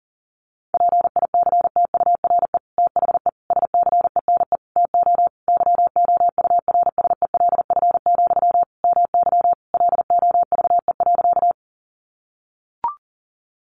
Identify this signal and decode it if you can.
Morse code, heard as a keyed tone, about 37 wpm, 700 Hz. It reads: PICTURE THE SCENE TO YOURSELF, MY LOVE.
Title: Sentences from Top 1100 Words 40wpm